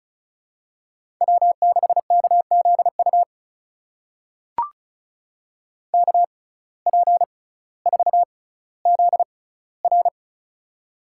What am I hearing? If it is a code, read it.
Morse code, heard as a keyed tone, about 36 wpm, spaced as for 18 wpm, 700 Hz. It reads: W6KZU KP4ZR
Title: USA Call Signs - 5 Characters - 6x Character Spacing - 35wpm